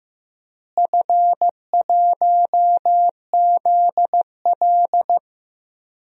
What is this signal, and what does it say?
Morse code, heard as a keyed tone, about 16 wpm, 700 Hz. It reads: F1ZL
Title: DXCC-100 Call Signs - 4 Characters - Repeated 2x - 15wpm